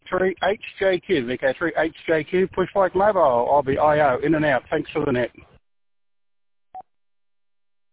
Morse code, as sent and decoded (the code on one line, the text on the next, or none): none